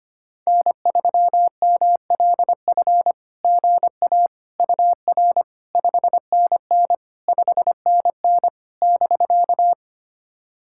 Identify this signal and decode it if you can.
Morse code, heard as a keyed tone, about 25 wpm, 660 Hz. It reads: N3MLF GA UR 5NN 5NN <BK>
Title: SOTA QSO Practice 25wpm